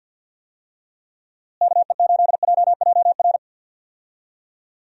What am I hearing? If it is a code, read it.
Morse code, heard as a keyed tone, about 49 wpm, 700 Hz. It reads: KE8JJR
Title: USA Call Signs - 6 Characters - Repeated 3x - 50wpm